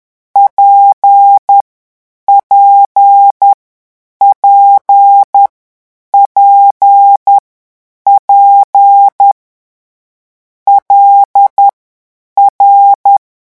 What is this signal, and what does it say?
Morse code, heard as a keyed tone, about 10 wpm, spaced as for 8 wpm, 800 Hz. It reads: PPPPP LR
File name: morse mp3 / lecon6-vitesse 10